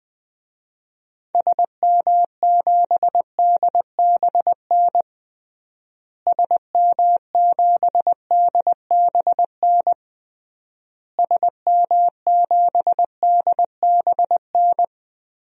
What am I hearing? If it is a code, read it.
Morse code, heard as a keyed tone, about 20 wpm, 700 Hz. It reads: SM7DBN SM7DBN SM7DBN
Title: DXCC-100 Call Signs - 6 Characters - Repeated 3x - 20wpm